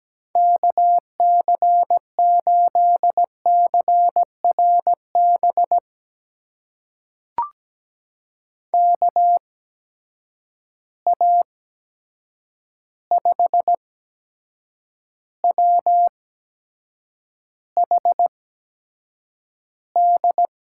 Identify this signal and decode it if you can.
Morse code, heard as a keyed tone, about 17 wpm, spaced as for 7 wpm, 700 Hz. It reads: KC8CRB KA5WHD
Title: USA Call Signs - 6 Characters - 8x Character Spacing - 17wpm